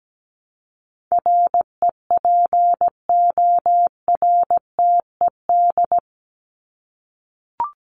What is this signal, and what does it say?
Morse code, heard as a keyed tone, about 17 wpm, 700 Hz. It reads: REPORTED
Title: ED Suffix - Top 100 Words with Base Words 17wpm